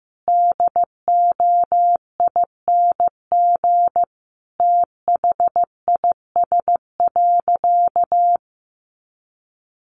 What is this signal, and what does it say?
Morse code, heard as a keyed tone, about 15 wpm, 700 Hz. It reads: DOING THIS.
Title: Easy Phrases 15wpm